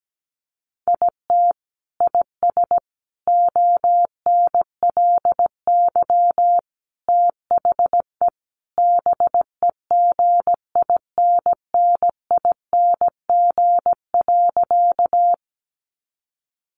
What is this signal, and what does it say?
Morse code, heard as a keyed tone, about 17 wpm, 700 Hz. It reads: IT IS ONLY THE BEGINNING.
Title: Sentences from Top 800 Words 17wpm